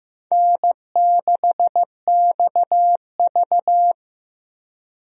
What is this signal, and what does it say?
Morse code, heard as a keyed tone, about 15 wpm, 700 Hz. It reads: N6XV
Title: USA Call Signs - 4 Characters - Repeated 3x - 15wpm